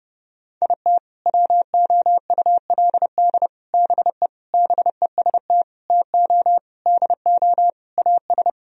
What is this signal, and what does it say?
Morse code, heard as a keyed tone, about 30 wpm, 700 Hz. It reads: IT WOULD BE BEST TO DO AS
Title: Sentences from Top 1000 Words - Review Part 2 - 30wpm